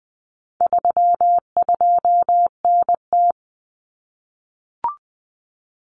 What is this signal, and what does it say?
Morse code, heard as a keyed tone, about 20 wpm, 700 Hz. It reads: 32NT
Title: Common Cut Numbers - 4-Digit 20wpm